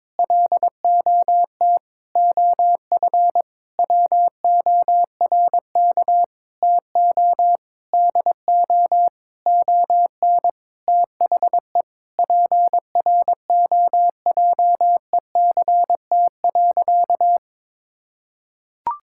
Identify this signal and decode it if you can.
Morse code, heard as a keyed tone, about 22 wpm, 700 Hz. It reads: LOT OF WORK TO DO ON THE PROJECT.
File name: Sentences with 9-Word Phrases - 22wpm